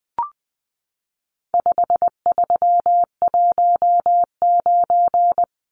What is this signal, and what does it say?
Morse code, heard as a keyed tone, about 20 wpm, 700 Hz. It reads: 5319
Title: Numbers - 4-Digit 20wpm